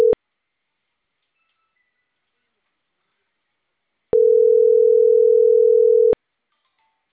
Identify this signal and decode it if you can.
Ringback tone (North America), 440 + 480 Hz.